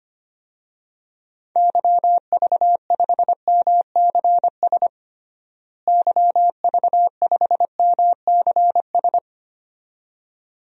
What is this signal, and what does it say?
Morse code, heard as a keyed tone, about 25 wpm, 700 Hz. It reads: YV5MCS YV5MCS
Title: DXCC-100 Call Signs - 6 Characters - Repeated 2x - 25wpm